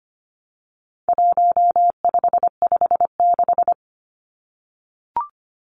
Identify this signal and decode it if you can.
Morse code, heard as a keyed tone, about 25 wpm, 700 Hz. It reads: 1556
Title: Numbers - 4-Digit 25wpm